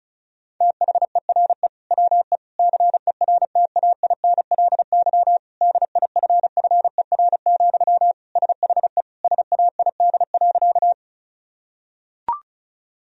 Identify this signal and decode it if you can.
Morse code, heard as a keyed tone, about 36 wpm, 700 Hz. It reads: THERE WE CERTAINLY DIFFER, SHE SAID.
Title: Sentences from Top 1000 Words - Review Part 1 - 35wpm